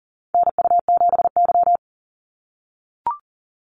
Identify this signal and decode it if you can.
Morse code, heard as a keyed tone, about 40 wpm, 700 Hz. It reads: NU7Y
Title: NA SOTA Activator Callsigns 40wpm